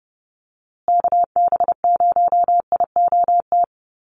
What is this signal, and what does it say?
Morse code, heard as a keyed tone, about 30 wpm, 700 Hz. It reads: KB0IOT
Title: USA Call Signs Repeated 2x 30wpm